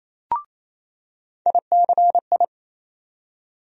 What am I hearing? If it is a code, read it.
Morse code, heard as a keyed tone, about 27 wpm, 700 Hz. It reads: ICI
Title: USA Call Sign Postfixes - 3 Letters 28wpm